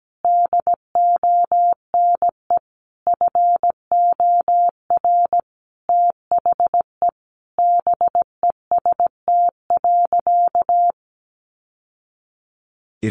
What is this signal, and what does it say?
Morse code, heard as a keyed tone, about 17 wpm, 700 Hz. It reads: DONE FOR THE BEST.
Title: Sentences from Top 600 Words 17wpm